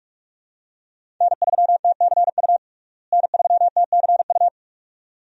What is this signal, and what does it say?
Morse code, heard as a keyed tone, about 47 wpm, 700 Hz. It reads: N3TCU N3TCU
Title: USA Call Signs - 5 Characters - Repeated 2x - 45wpm